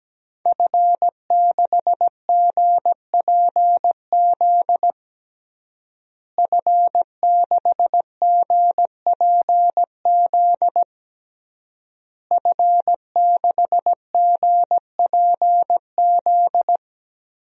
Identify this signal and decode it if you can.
Morse code, heard as a keyed tone, about 17 wpm, 700 Hz. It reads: F6GPZ F6GPZ F6GPZ
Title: DXCC-100 Call Signs - 5 Characters - Repeated 3x - 17wpm